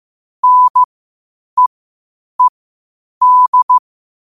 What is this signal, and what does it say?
Morse code, heard as a keyed tone, about 15 wpm, spaced as for 8 wpm, 1000 Hz. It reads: NEED
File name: qotdfiles / 8wpm